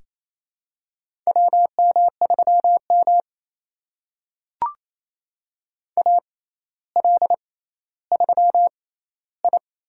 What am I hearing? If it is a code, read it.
Morse code, heard as a keyed tone, about 27 wpm, spaced as for 14 wpm, 700 Hz. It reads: WM3M AL3I
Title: USA Call Signs - 4 Characters - 6x Character Spacing - 28wpm